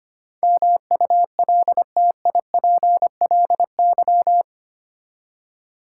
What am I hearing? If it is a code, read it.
Morse code, heard as a keyed tone, about 25 wpm, 700 Hz. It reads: MULTIPLY
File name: Eight Letter Words - 25wpm